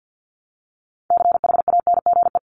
Morse code, heard as a keyed tone, about 48 wpm, 700 Hz. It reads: CHANGE
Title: ING Suffix - Top 100 Words with Base Words 50wpm